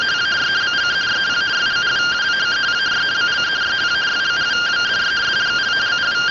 Two-tone teleprinter signal, about 170 Hz shift.